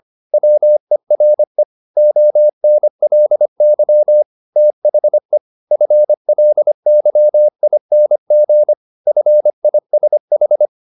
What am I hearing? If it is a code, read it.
Morse code, heard as a keyed tone, about 25 wpm, 600 Hz. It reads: WERE ONLY THE FLYING FISH